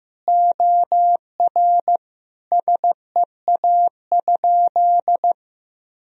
Morse code, heard as a keyed tone, about 15 wpm, 700 Hz. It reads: OR SEA?
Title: Sentences from Top 1000 Words - Review Part 1 - 15wpm